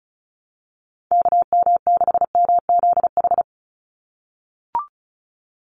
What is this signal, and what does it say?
Morse code, heard as a keyed tone, about 34 wpm, 700 Hz. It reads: KM6MZH
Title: USA Call Signs 35wpm